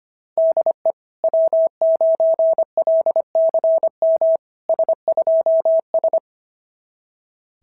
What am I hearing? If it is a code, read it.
Morse code, heard as a keyed tone, about 25 wpm, 690 Hz. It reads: DE W9LCM S2S
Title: SOTA QSO Practice 25wpm